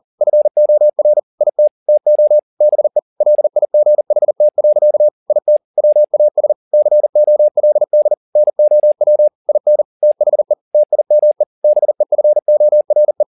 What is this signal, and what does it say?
Morse code, heard as a keyed tone, about 40 wpm, 600 Hz. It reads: FOR IT TO BE LIGHT. IT WAS COLD NOW IN THE TIME BEFORE